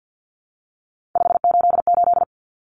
Morse code, heard as a keyed tone, about 50 wpm, 700 Hz. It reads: 588
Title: Numbers - 3-Digit 50wpm